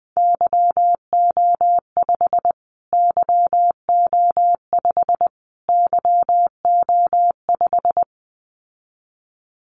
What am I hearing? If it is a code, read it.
Morse code, heard as a keyed tone, about 20 wpm, 700 Hz. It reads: YO5 YO5 YO5
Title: Callsign Prefixes for Top 25 DXCC - Repeated 3x - 20wpm